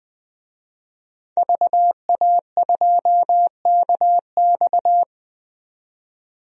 Morse code, heard as a keyed tone, about 20 wpm, 700 Hz. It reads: VA2KX
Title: DXCC-100 Call Signs - 5 Characters - 20wpm